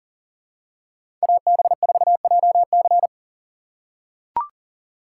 Morse code, heard as a keyed tone, about 38 wpm, 700 Hz. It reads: AB4JC
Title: USA Call Signs - 5 Characters - 4x Character Spacing - 40wpm